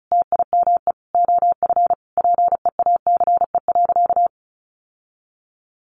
Morse code, heard as a keyed tone, about 34 wpm, 700 Hz. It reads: TIME OF PEACE.
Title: Sentences from Top 1500 Words 35wpm